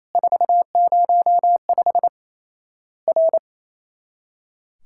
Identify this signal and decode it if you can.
Morse code, heard as a keyed tone, about 28 wpm, 700 Hz. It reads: 405 R
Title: Worked All Europe DX Contest - QTC Exchange - 28wpm